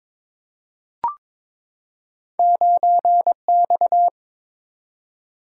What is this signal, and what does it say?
Morse code, heard as a keyed tone, about 22 wpm, 700 Hz. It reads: 9X
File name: Two Letter-Number Combinations - 22wpm